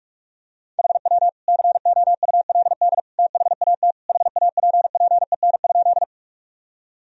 Morse code, heard as a keyed tone, about 42 wpm, 700 Hz. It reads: HW COULD THAT HAPPEN?
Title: Sentences from Top 400 Words and Abbreviations 45wpm